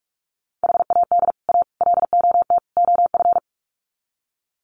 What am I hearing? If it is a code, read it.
Morse code, heard as a keyed tone, about 44 wpm, 700 Hz. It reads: HAD A LOT OF
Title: Sets of 4 Words 45wpm